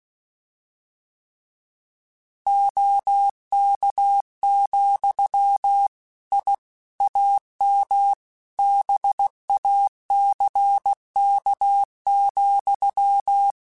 Morse code, heard as a keyed tone, about 16 wpm, 800 Hz. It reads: OK, I AM BACK,
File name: QSOs / QSOs-mp3s / kc7sac@15